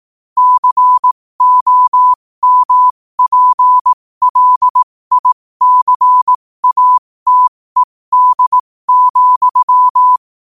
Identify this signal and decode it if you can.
Morse code, heard as a keyed tone, about 18 wpm, spaced as for 15 wpm, 1000 Hz. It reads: COMPLICATED,
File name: qotdfiles / 15wpm